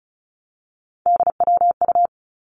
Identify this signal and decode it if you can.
Morse code, heard as a keyed tone, about 36 wpm, 700 Hz. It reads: DWU
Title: Three Challenging Letter-Number Combinations 35wpm